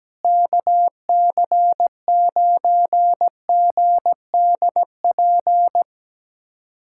Morse code, heard as a keyed tone, about 17 wpm, 700 Hz. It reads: KC9GDP
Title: USA Callsign DE Callsign 17wpm